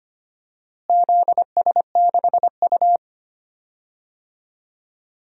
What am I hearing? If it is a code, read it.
Morse code, heard as a keyed tone, about 25 wpm, 700 Hz. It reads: ZS6U
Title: DXCC-100 Call Signs - 4 Characters - Repeated 2x - 25wpm